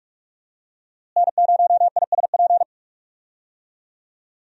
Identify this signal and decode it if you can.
Morse code, heard as a keyed tone, about 46 wpm, 700 Hz. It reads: N0ISP
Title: USA Call Signs - 5 Characters - Repeated 3x - 45wpm